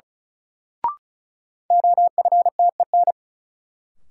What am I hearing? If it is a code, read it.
Morse code, heard as a keyed tone, about 35 wpm, 700 Hz. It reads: OFTEN
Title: Sentences from Top 300 Words - New Word Drill - 35wpm